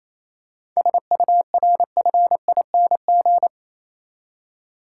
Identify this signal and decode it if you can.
Morse code, heard as a keyed tone, about 27 wpm, 700 Hz. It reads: SURFING
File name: Easy Words - Speed-Racing with Hints - 22wpm